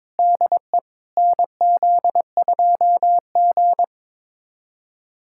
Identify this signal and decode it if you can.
Morse code, heard as a keyed tone, about 22 wpm, 700 Hz. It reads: DE NZ2G
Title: USA Callsign DE Callsign - 4 Characters - 22wpm